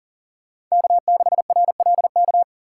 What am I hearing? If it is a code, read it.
Morse code, heard as a keyed tone, about 40 wpm, 700 Hz. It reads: K6RLK
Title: USA Call Signs - 5 Characters - 3x Character Spacing - 40wpm